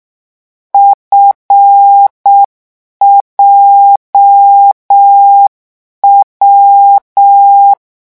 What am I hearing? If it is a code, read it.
Morse code, heard as a keyed tone, about 6 wpm, 800 Hz. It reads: FJW